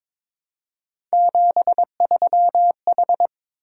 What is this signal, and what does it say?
Morse code, heard as a keyed tone, about 22 wpm, 700 Hz. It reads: 73H
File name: practice / 22wpm / Three Letter-Number Combinations - 22wpm